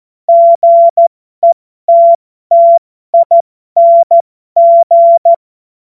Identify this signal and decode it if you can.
Morse code, heard as a keyed tone, about 14 wpm, spaced as for 12 wpm, 680 Hz. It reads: GETTING
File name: Morsekurs / Witze